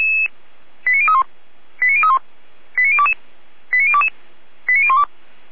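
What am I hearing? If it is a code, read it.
Five-tone selective call (ZVEI), heard as quick runs of five tones, tones of about 70 ms. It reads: 89031 89031 89022 89022 89012
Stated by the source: ZVEI five-tone call 89001 received by SDR, from the server of Feuerwehr Orlamünde (fire brigade).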